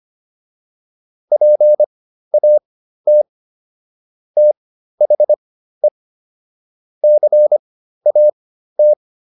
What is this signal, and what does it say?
Morse code, heard as a keyed tone, about 25 wpm, spaced as for 13 wpm, 600 Hz. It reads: PAT THE CAT